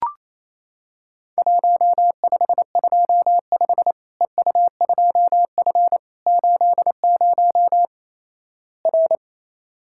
Morse code, heard as a keyed tone, about 28 wpm, 700 Hz. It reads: 1525 EU2F 80 R
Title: Worked All Europe DX Contest - QTC Exchange - 28wpm